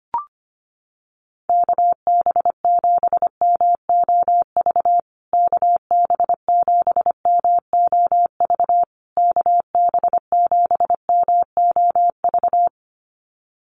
Morse code, heard as a keyed tone, about 25 wpm, 700 Hz. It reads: KB7MOV KB7MOV KB7MOV
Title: USA Call Signs Repeated 3x 25wpm